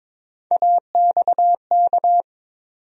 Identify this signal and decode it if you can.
Morse code, heard as a keyed tone, about 22 wpm, 700 Hz. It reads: AXK
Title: USA Call Sign Postfixes - 3 Letters 22wpm